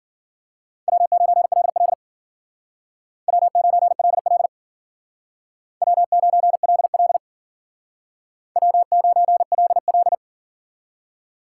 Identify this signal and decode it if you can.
Morse code, heard as a keyed tone, about 51 wpm, 700 Hz. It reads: W9LL W9LL W9LL W9LL